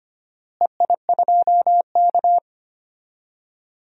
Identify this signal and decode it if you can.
Morse code, heard as a keyed tone, about 25 wpm, 700 Hz. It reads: EI2K
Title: DXCC-100 Call Signs - 4 Characters - Repeated 3x - 25wpm